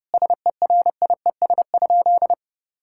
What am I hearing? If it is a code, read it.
Morse code, heard as a keyed tone, about 30 wpm, 700 Hz. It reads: SERIES?
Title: Sentences from Top 1000 Words - Review Part 5 - 30wpm